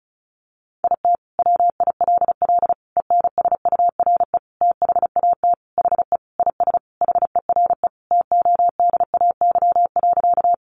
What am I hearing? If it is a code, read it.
Morse code, heard as a keyed tone, about 34 wpm, 700 Hz. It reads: IT WILL ENSURE THAT HE IS HERE TODAY.
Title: Sentences from Top 1100 Words 35wpm